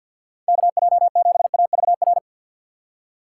Morse code, heard as a keyed tone, about 52 wpm, 700 Hz. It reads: KJ7AVR
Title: USA Call Signs - 2x Character Spacing - 50wpm